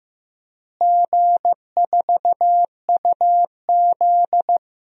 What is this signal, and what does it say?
Morse code, heard as a keyed tone, about 15 wpm, 700 Hz. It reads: G4UZ
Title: DXCC-100 Call Signs - 4 Characters - Repeated 2x - 15wpm